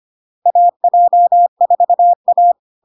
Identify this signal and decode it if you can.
Morse code, heard as a keyed tone, about 25 wpm, 700 Hz. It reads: AJ4A
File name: Log-in-CW-25wpm-700-Hz